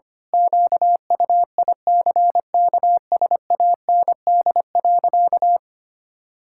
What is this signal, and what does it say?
Morse code, heard as a keyed tone, about 25 wpm, 700 Hz. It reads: QUICKSAND.
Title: Pangram Challenge v2 - 25wpm